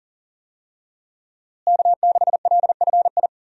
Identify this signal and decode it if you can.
Morse code, heard as a keyed tone, about 37 wpm, 700 Hz. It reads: K6LFI
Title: USA Call Signs - 5 Characters - Repeated 3x - 40wpm